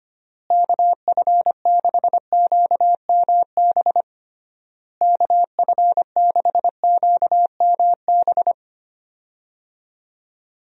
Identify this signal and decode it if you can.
Morse code, heard as a keyed tone, about 25 wpm, 700 Hz. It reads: KF6QMB KF6QMB
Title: USA Call Signs - 6 Characters - Repeated 3x - 25wpm